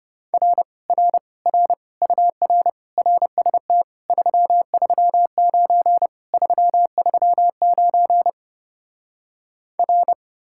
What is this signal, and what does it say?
Morse code, heard as a keyed tone, about 33 wpm, 700 Hz. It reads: R R R UR RST 339 339 R